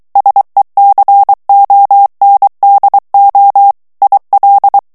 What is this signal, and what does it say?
Morse code, heard as a keyed tone, about 23 wpm, 800 Hz. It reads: SECONDO IL
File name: ITALIEN texte clair 2 vitesse 22 mots minute